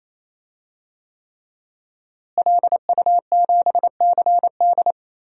Morse code, heard as a keyed tone, about 28 wpm, 700 Hz. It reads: LU7CD
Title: DXCC-100 Call Signs - 5 Characters - Repeated 3x - 28wpm